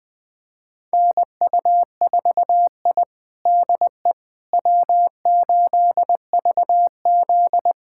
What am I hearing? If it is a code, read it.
Morse code, heard as a keyed tone, about 21 wpm, 700 Hz. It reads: NU4I DE W8VZ
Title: USA Callsign DE Callsign - 4 Characters - 20wpm